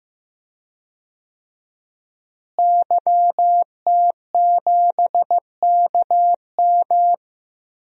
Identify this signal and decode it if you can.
Morse code, heard as a keyed tone, about 15 wpm, 700 Hz. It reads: YT7KM
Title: DXCC-100 Call Signs - 5 Characters - Repeated 2x - 15wpm